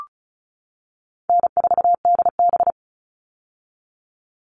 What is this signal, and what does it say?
Morse code, heard as a keyed tone, about 34 wpm, 700 Hz. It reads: N4DB
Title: All Cut Numbers - 4-Digit 35wpm